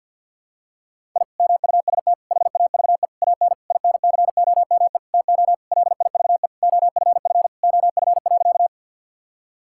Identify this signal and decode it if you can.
Morse code, heard as a keyed tone, about 52 wpm, 700 Hz. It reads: I MUST HAVE AN INCOME TO LIVE OFF OF.
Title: Sentences from Top 1000 Words - Review Part 5 - 50wpm